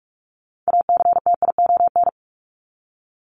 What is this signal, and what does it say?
Morse code, heard as a keyed tone, about 44 wpm, 700 Hz. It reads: ACTION
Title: Top 501-750 Words 45wpm